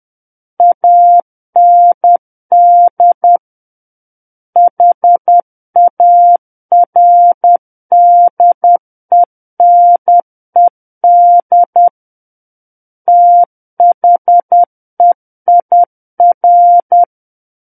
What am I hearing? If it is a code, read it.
Morse code, heard as a keyed tone, about 10 wpm, 700 Hz. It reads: AND HARDENED THEIR
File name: Book One / 01-Chapter 1 - 10 WPM